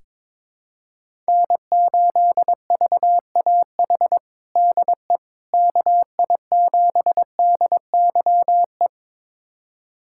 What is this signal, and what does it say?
Morse code, heard as a keyed tone, about 22 wpm, 700 Hz. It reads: N8VAH DE KI7DYE
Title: USA Callsign DE Callsign 22wpm